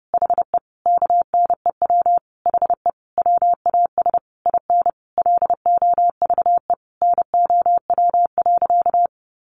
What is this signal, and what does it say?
Morse code, heard as a keyed tone, about 30 wpm, 700 Hz. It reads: HE KNEW HE WAS IN LOVE NOW.
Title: Sentences from Top 1300 Words 30wpm